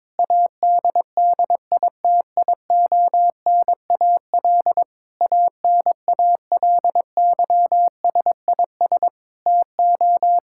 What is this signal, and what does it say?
Morse code, heard as a keyed tone, about 22 wpm, 700 Hz. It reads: ADDITIONAL ANALYSIS TO
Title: Sentences from Top 1000 Words - Review Part 5 - 22wpm